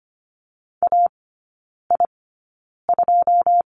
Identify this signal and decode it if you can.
Morse code, heard as a keyed tone, about 25 wpm, spaced as for 9 wpm, 700 Hz. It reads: AI2
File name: Lesson 17e - Letter u - Three Characters 25wpm